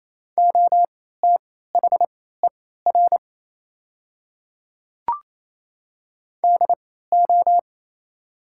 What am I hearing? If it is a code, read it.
Morse code, heard as a keyed tone, about 28 wpm, spaced as for 16 wpm, 700 Hz. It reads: OTHER DO
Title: 3x Character Spacing - Top 300 Words - 28wpm